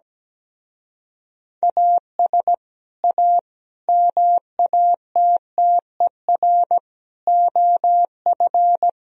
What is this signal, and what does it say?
Morse code, heard as a keyed tone, about 17 wpm, 700 Hz. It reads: AS A MATTER OF
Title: Speed-Racing - Sets of 5 Words 17wpm